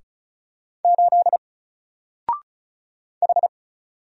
Morse code, heard as a keyed tone, about 33 wpm, 700 Hz. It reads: 8 H
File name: Single-Letter-Number-35wpm